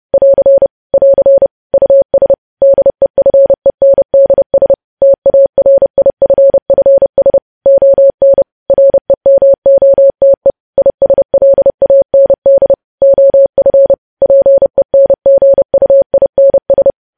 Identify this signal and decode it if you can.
Morse code, heard as a keyed tone, about 30 wpm, 550 Hz. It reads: <AR> <AR> US DEFENDS TARIFFS ON REMOTE ISLAND OF PENGUINS
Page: news headlines in morse code at 30 wpm